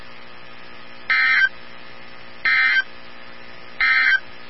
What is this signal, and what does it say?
Emergency Alert System (SAME) tones, heard as harsh buzzing bursts. It is heard from 1.0 to 4.3 s.
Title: NOAA Weather Radio 162.550